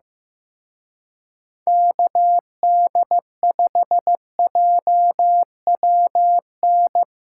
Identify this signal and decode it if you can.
Morse code, heard as a keyed tone, about 17 wpm, 700 Hz. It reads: KD5JWN
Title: Speed Racing - USA Call Signs 15wpm